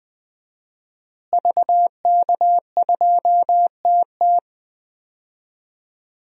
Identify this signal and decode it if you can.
Morse code, heard as a keyed tone, about 21 wpm, 700 Hz. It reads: VK2TT
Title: DXCC-100 Call Signs - 5 Characters - Repeated 3x - 20wpm